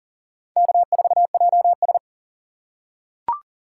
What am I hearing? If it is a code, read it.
Morse code, heard as a keyed tone, about 40 wpm, 700 Hz. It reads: K4JS
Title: USA Call Signs - 4 Characters - 4x Character Spacing - 40wpm